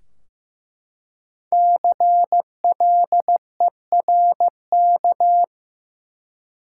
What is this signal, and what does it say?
Morse code, heard as a keyed tone, about 15 wpm, 700 Hz. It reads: CLERK
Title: Five Letter Words - 15wpm